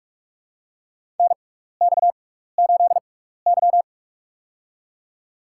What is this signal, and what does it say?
Morse code, heard as a keyed tone, about 46 wpm, spaced as for 16 wpm, 700 Hz. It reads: NX8Y
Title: USA Call Signs - 6x Character Spacing - 45wpm